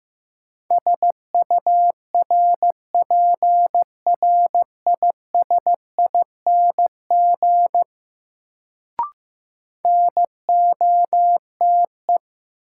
Morse code, heard as a keyed tone, about 16 wpm, 700 Hz. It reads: SURPRISING NOTE
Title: Sets of 5 Words - New Word Drill - 15wpm